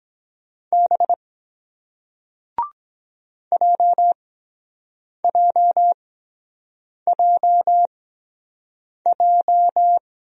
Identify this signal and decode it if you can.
Morse code, heard as a keyed tone, about 23 wpm, spaced as for 7 wpm, 700 Hz. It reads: B JJJJ